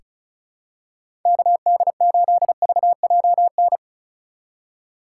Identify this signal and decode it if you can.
Morse code, heard as a keyed tone, about 36 wpm, 700 Hz. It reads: KD8VJN
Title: USA Call Signs - 6 Characters - 4x Character Spacing - 35wpm